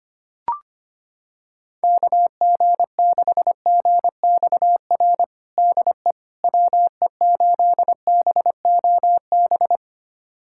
Morse code, heard as a keyed tone, about 25 wpm, 700 Hz. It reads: KG6GXR DE WE8BOB
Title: USA Callsign DE Callsign 25wpm